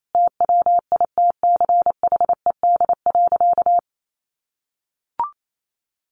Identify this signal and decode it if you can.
Morse code, heard as a keyed tone, about 28 wpm, 700 Hz. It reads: TWITCHED.
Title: A Sound Of Thunder 28wpm